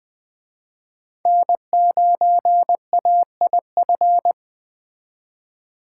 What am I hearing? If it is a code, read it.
Morse code, heard as a keyed tone, about 21 wpm, 700 Hz. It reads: N9AIF
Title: Speed Racing - USA 1x3 Call Signs 17wpm